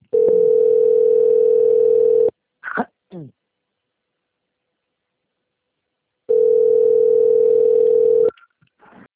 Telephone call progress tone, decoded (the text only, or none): ringback tone